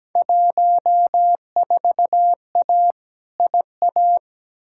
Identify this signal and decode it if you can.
Morse code, heard as a keyed tone, about 17 wpm, 700 Hz. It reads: 14A IA
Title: Field Day Exchanges Encore - 17wpm